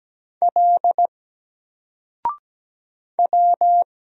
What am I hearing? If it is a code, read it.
Morse code, heard as a keyed tone, about 17 wpm, 700 Hz. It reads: L W
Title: Single Letter-Number 17wpm